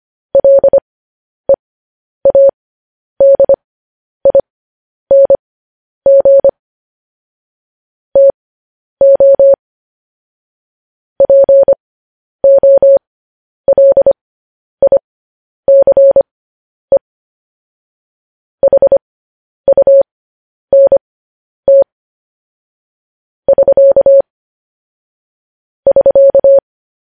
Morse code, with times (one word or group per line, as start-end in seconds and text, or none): 0.3-6.5 LEADING
8.2-9.5 TO
11.2-17.0 POLICE
18.6-21.8 HUNT
23.5-24.2 <SK>
25.9-26.6 <SK>